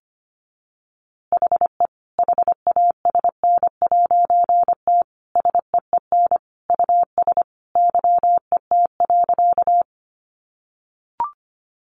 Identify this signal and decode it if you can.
Morse code, heard as a keyed tone, about 25 wpm, 700 Hz. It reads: HE HASN'T SEEN US YET.
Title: A Sound Of Thunder 25wpm